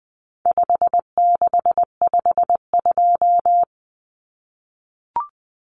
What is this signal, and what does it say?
Morse code, heard as a keyed tone, about 20 wpm, 700 Hz. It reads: 5652
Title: Common Cut Numbers - 4-Digit 20wpm